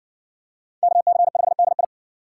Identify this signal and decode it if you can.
Morse code, heard as a keyed tone, about 57 wpm, 700 Hz. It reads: KC5DI